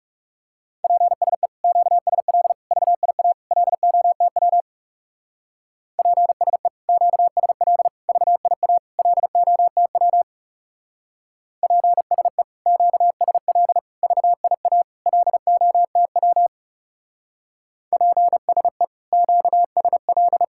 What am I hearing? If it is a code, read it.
Morse code, heard as a keyed tone, about 37 wpm, 700 Hz. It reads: PSE QSL VIA LOTW PSE QSL VIA LOTW PSE QSL VIA LOTW PSE QSL
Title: Speed-Racing Common QSO Phrases - 30wpm